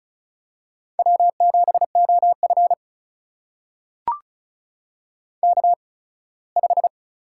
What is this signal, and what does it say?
Morse code, heard as a keyed tone, about 36 wpm, spaced as for 21 wpm, 700 Hz. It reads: W7OF K5
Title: USA Call Signs - 4 Characters - 8x Character Spacing - 35wpm